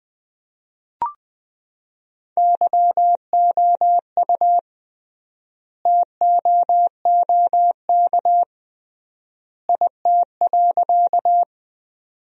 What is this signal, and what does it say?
Morse code, heard as a keyed tone, about 20 wpm, 700 Hz. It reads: YOU TOOK IT.
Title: 3x Word-Spacing - Sentences from Top 300 Words - 20wpm